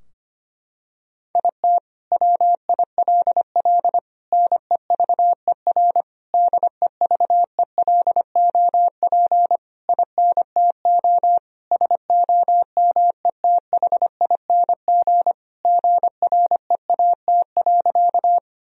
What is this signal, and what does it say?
Morse code, heard as a keyed tone, about 25 wpm, 700 Hz. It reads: IT WILL NEVER DEVELOP INTO SOMETHING GREAT.
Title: Sentences from Top 1000 Words - Review Part 2 - 25wpm